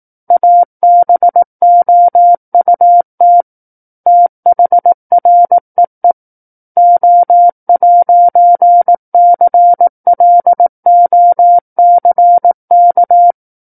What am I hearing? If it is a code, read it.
Morse code, heard as a keyed tone, about 18 wpm, 700 Hz. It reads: ABOUT THREE O'CLOCK
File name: 18 WPM